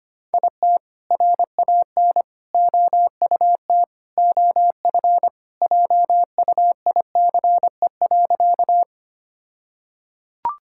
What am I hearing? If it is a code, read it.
Morse code, heard as a keyed tone, about 25 wpm, 700 Hz. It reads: IT RAN OUT OF JUICE.
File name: practice / 25wpm / All-Age Appropriate Jokes - 25wpm